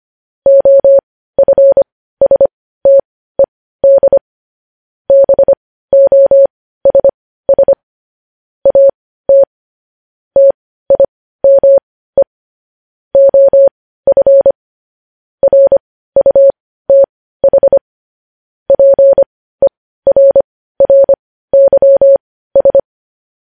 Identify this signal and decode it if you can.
Morse code, heard as a keyed tone, about 25 wpm, spaced as for 15 wpm, 550 Hz. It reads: OFSTED BOSS AT TIME OF RUTH PERRYS
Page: News Headlines in Morse Code at 15 WPM